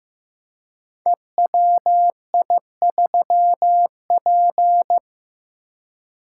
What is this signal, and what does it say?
Morse code, heard as a keyed tone, about 15 wpm, 700 Hz. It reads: EWI3P
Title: Code Groups - 5 Letter:Numbers - 15wpm